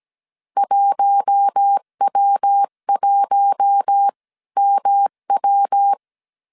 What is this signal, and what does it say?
Morse code, heard as a keyed tone, about 17 wpm, spaced as for 15 wpm, 800 Hz. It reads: 1W1MW
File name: CW / DET / 1 6 1500C S C 16wpm 2733